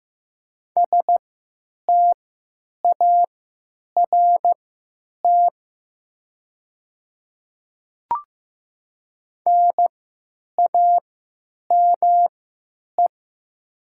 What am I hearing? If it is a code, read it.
Morse code, heard as a keyed tone, about 15 wpm, spaced as for 9 wpm, 700 Hz. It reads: START NAME
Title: 3x Character Spacing - Top 300 Words - 15wpm